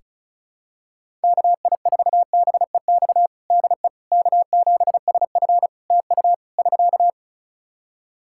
Morse code, heard as a keyed tone, about 36 wpm, 700 Hz. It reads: KI4BEX DE K7SF TU <SK>
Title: Practice QSO - Exchange 4 - Ending - 35wpm